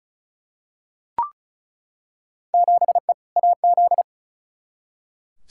Morse code, heard as a keyed tone, about 34 wpm, 700 Hz. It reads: 7E AZ
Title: Field Day Exchanges Encore - 35wpm